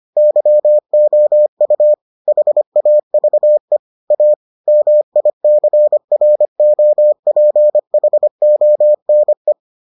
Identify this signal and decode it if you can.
Morse code, heard as a keyed tone, about 25 wpm, 600 Hz. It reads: YOU HAVE A MICROPHONE